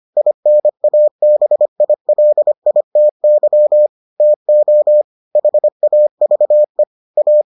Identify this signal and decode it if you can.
Morse code, heard as a keyed tone, about 25 wpm, 600 Hz. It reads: INABILITY TO HAVE A